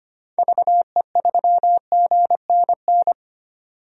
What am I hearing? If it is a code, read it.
Morse code, heard as a keyed tone, about 25 wpm, 690 Hz. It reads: VE3GNN
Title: DXCC-100 Call Signs - 6 Characters - 25wpm